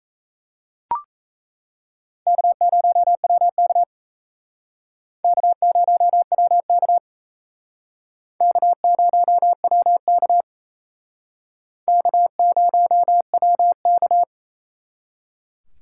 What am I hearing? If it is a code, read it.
Morse code, heard as a keyed tone, about 35 wpm, 700 Hz. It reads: K0WK K0WK K0WK K0WK